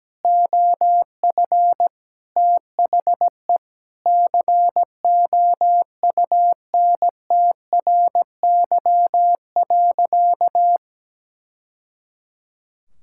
Morse code, heard as a keyed tone, about 17 wpm, 700 Hz. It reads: OF THE COUNTRY.
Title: Sentences from Top 1000 Words - Review Part 5 - 17wpm